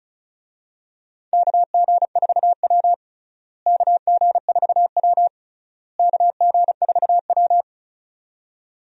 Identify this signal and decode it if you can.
Morse code, heard as a keyed tone, about 36 wpm, 700 Hz. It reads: KG4W KG4W KG4W
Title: USA Call Signs - 4 Characters - Repeated 3x - 35wpm